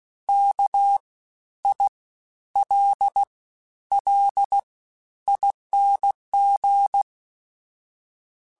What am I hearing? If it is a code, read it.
Morse code, heard as a keyed tone, about 16 wpm, 800 Hz. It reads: K I L L ING
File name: Special INGs-mp3s / Special INGs-mp3s / SlowToFaster ing @15